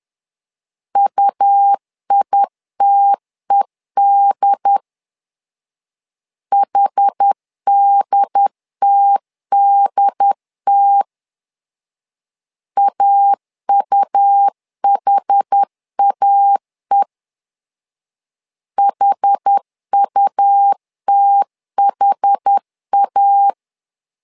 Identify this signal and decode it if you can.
Morse code, heard as a keyed tone, about 10 wpm, 800 Hz. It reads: UITED HDTDT AUHAE HUTHA
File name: LEZIONI / 2 UHDN / 2 UHD 10wpm 1826